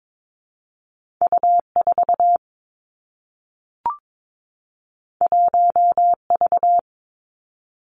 Morse code, heard as a keyed tone, about 22 wpm, 700 Hz. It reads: U 4 1 V
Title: Two Challenging Letter-Number Combinations 22wpm